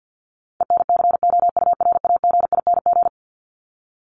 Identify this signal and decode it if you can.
Morse code, heard as a keyed tone, about 50 wpm, 700 Hz. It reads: ENCOURAGING
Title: Top 1751-2000 Words 50wpm